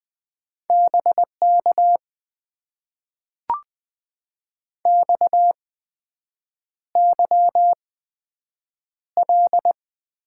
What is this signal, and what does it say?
Morse code, heard as a keyed tone, about 20 wpm, spaced as for 7 wpm, 700 Hz. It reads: BK XYL
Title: Top 90 QSO Elements - 8x Character Spacing - 20wpm